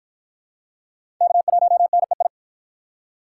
Morse code, heard as a keyed tone, about 49 wpm, 700 Hz. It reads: K1NEI